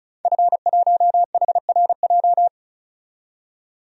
Morse code, heard as a keyed tone, about 36 wpm, 700 Hz. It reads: F1HRJ
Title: DXCC-100 Call Signs - 5 Characters - 35wpm